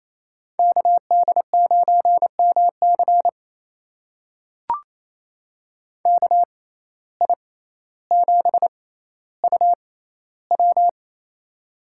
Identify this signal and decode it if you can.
Morse code, heard as a keyed tone, about 28 wpm, spaced as for 14 wpm, 700 Hz. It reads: KD9MC KI7UW
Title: USA Call Signs - 5 Characters - 6x Character Spacing - 28wpm